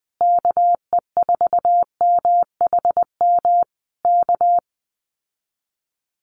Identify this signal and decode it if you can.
Morse code, heard as a keyed tone, about 20 wpm, 700 Hz. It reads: KE4MHM K
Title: Common QSO Phrases Encore 20wpm